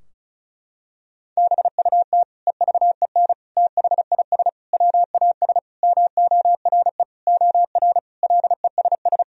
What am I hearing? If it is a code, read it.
Morse code, heard as a keyed tone, about 34 wpm, 700 Hz. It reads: BUT EVEN THIS WAS MORE OR LESS
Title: Sentences from Top 1000 Words - Review Part 2 - 35wpm